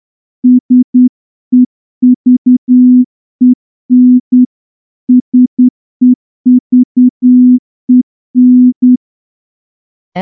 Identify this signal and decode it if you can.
Morse code, heard as a keyed tone, about 9 wpm, 250 Hz. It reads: SEVEN SEVEN
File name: morse-training-10WPM